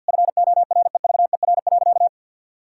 Morse code, heard as a keyed tone, about 50 wpm, 700 Hz. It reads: FOREVER.
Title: Sentences from Top 1600 Words 50wpm